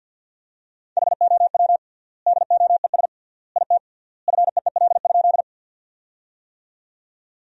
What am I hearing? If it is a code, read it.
Morse code, heard as a keyed tone, about 50 wpm, 700 Hz. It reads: HOW DOES IT FEEL?
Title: 3x Word-Spacing - Sentences from Top 400 Words - 50wpm